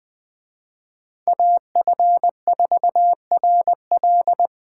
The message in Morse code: AF4RL

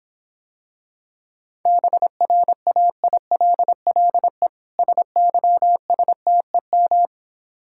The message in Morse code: BRAILLE SYSTEM

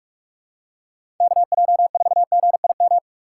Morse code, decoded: KJ4GIM

